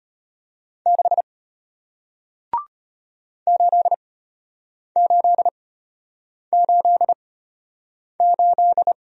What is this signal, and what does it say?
Morse code, heard as a keyed tone, about 33 wpm, spaced as for 8 wpm, 700 Hz. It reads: 6 8888